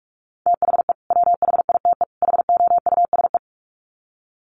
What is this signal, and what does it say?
Morse code, heard as a keyed tone, about 44 wpm, 700 Hz. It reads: THE WHITE HOUSE
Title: Sets of 3 Words 45wpm